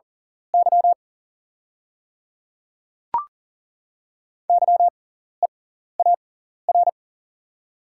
Morse code, heard as a keyed tone, about 41 wpm, spaced as for 14 wpm, 700 Hz. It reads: Y YEAR
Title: 6x Character Spacing - Top 500 Words - 40wpm